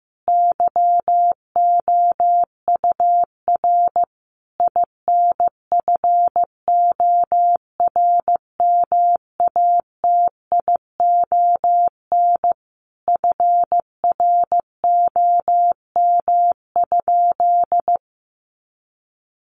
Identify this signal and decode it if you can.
Morse code, heard as a keyed tone, about 15 wpm, 700 Hz. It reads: YOUR INFORMATION FROM?
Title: Sentences from Top 600 Words 15wpm